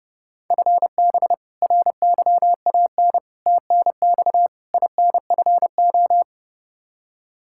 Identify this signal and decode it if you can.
Morse code, heard as a keyed tone, about 32 wpm, 700 Hz. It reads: FB RYAN TNX INFO
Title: Practice QSO - Exchange 3 - More Info - 30wpm